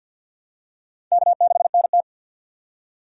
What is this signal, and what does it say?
Morse code, heard as a keyed tone, about 48 wpm, 700 Hz. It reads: K6NT
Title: USA Call Signs - 4 Characters - 6x Character Spacing - 50wpm